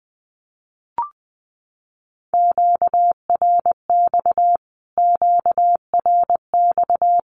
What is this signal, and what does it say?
Morse code, heard as a keyed tone, about 19 wpm, 700 Hz. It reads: QRX QRX
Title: Q-Codes Repeated 3x 20wpm